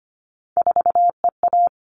Morse code, heard as a keyed tone, about 25 wpm, 700 Hz. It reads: 4EA
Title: All Cut Numbers - 3-Digit 25wpm